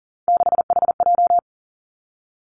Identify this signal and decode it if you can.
Morse code, heard as a keyed tone, about 42 wpm, 700 Hz. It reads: 6HJ
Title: Three Challenging Letter-Number Combinations 40wpm